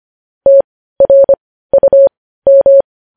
Morse code, heard as a keyed tone, about 25 wpm, spaced as for 15 wpm, 550 Hz. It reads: TRUM
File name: podcast / News headlines-1x-25x15